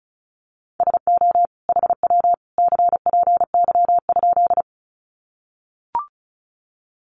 Morse code, heard as a keyed tone, about 34 wpm, 700 Hz. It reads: SO HW CPY?
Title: Common QSO Phrases 35wpm